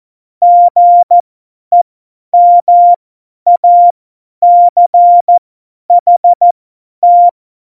Morse code, heard as a keyed tone, about 14 wpm, spaced as for 10 wpm, 700 Hz. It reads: GEMACHT
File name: Morsekurs / Witze / Witz 12. November 2019, 50 F70